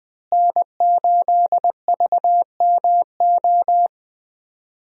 Morse code, heard as a keyed tone, about 20 wpm, 700 Hz. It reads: N8VMO
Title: USA Call Signs - 5 Characters - Repeated 3x - 20wpm